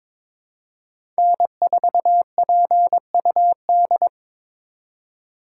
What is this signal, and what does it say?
Morse code, heard as a keyed tone, about 22 wpm, 700 Hz. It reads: N4PUD
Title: USA Call Signs - 5 Characters - 4x Character Spacing - 22wpm